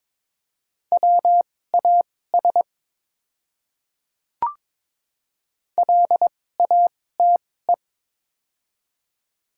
Morse code, heard as a keyed tone, about 22 wpm, spaced as for 16 wpm, 700 Hz. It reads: WAS LATE
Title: 2x Character Spacing - Top 300 Words - 22wpm